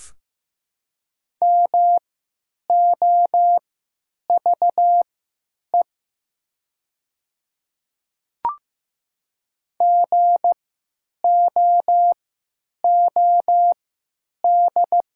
Morse code, heard as a keyed tone, about 15 wpm, spaced as for 9 wpm, 700 Hz. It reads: MOVE GOOD
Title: 3x Character Spacing - Top 200 Words - 15wpm